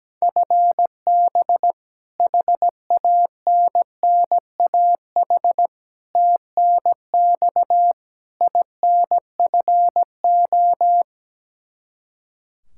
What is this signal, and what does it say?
Morse code, heard as a keyed tone, about 17 wpm, 700 Hz. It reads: FB HANNAH TNX INFO
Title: Practice QSO - Exchange 3 - More Info - 17wpm